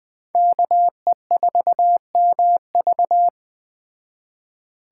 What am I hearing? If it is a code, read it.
Morse code, heard as a keyed tone, about 20 wpm, 700 Hz. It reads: KE4MV